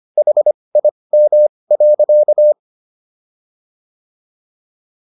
Morse code, heard as a keyed tone, about 25 wpm, spaced as for 20 wpm, 600 Hz. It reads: HIM.
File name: SS123 20